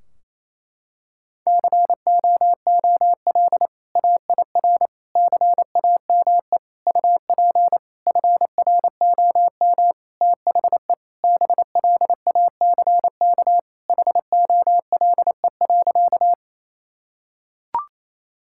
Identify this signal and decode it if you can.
Morse code, heard as a keyed tone, about 28 wpm, 700 Hz. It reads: COOL AIR CAME UP FROM THE BLACK HOLE.
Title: Sentences from Top 1600 Words 28wpm